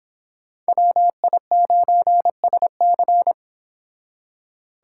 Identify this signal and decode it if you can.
Morse code, heard as a keyed tone, about 30 wpm, 700 Hz. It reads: WI9SC